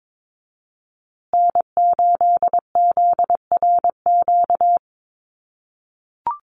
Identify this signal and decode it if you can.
Morse code, heard as a keyed tone, about 22 wpm, 700 Hz. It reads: N8ZRQ
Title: USA Call Signs Repeated 3x 22wpm